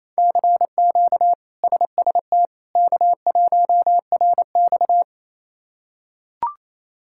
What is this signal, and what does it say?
Morse code, heard as a keyed tone, about 27 wpm, 700 Hz. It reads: CQ SST K1RX
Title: K1USN SST QSO Practice - 28wpm